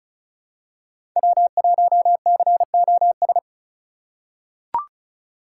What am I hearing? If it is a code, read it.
Morse code, heard as a keyed tone, about 36 wpm, 700 Hz. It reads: W1COS
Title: USA Call Signs - 5 Characters - 3x Character Spacing - 35wpm